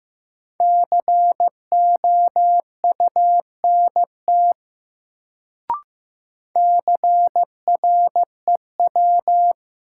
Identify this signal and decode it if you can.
Morse code, heard as a keyed tone, about 15 wpm, 700 Hz. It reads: COUNT CREW
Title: Sentences from Top 1500 Words - New Word Drill Alt - 15wpm